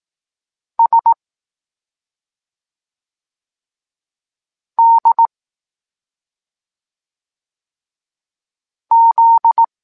Morse code, heard as a keyed tone, about 18 wpm, 920 Hz. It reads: S D Z